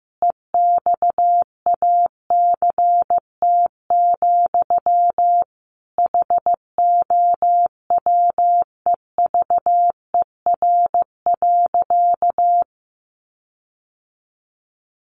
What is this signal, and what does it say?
Morse code, heard as a keyed tone, about 15 wpm, 700 Hz. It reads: EXACT, HOWEVER.
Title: Sentences from Top 1300 Words 15wpm